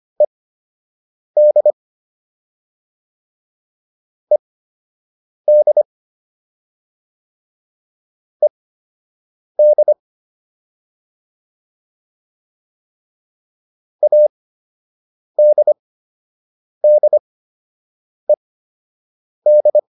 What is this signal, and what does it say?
Morse code, heard as a keyed tone, about 25 wpm, spaced as for 7 wpm, 600 Hz. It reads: ED ED ED ADDED